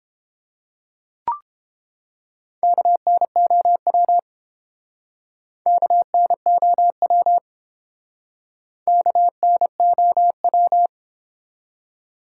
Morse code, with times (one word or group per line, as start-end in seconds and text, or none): 2.6-4.2 KNOW
5.7-7.4 KNOW
8.9-10.9 KNOW